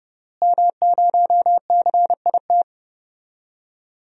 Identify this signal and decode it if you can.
Morse code, heard as a keyed tone, about 32 wpm, 700 Hz. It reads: M0CIT
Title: DXCC-100 Call Signs - 5 Characters - Repeated 3x - 30wpm